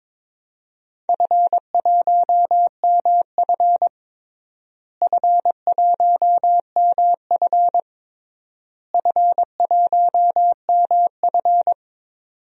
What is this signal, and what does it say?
Morse code, heard as a keyed tone, about 22 wpm, 700 Hz. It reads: F1MF F1MF F1MF
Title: DXCC-100 Call Signs - 4 Characters - Repeated 3x - 22wpm